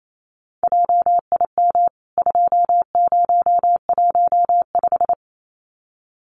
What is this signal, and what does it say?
Morse code, heard as a keyed tone, about 28 wpm, 700 Hz. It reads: JIM 2015
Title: CWT Exchange 28wpm